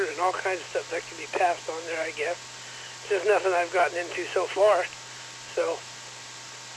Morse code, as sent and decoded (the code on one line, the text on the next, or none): none